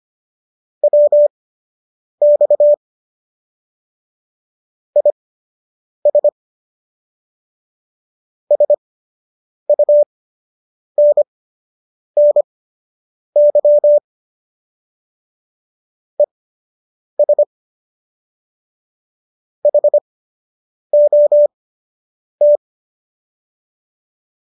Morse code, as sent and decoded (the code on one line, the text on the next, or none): .-- -..- / .. ... / ... ..- -. -. -.-- / . ... / .... --- -
WX IS SUNNY ES HOT